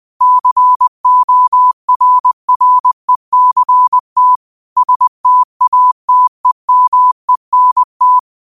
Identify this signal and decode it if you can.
Morse code, heard as a keyed tone, about 20 wpm, 1000 Hz. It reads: CORRECT STATEMENT
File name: qotdfiles / 20wpm